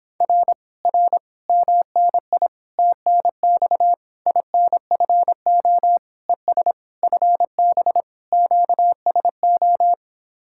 Morse code, heard as a keyed tone, about 26 wpm, 700 Hz. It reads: R R MNI TNX INFO ES FB QSO